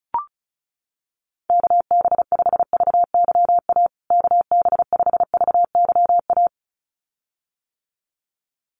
Morse code, heard as a keyed tone, about 34 wpm, 700 Hz. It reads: KB5VYA KB5VYA
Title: USA Call Signs Repeated 2x 35wpm